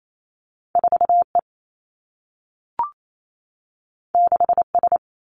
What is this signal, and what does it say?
Morse code, heard as a keyed tone, about 28 wpm, 700 Hz. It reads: 4 E 6 S
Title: Two Challenging Letter-Number Combinations 28wpm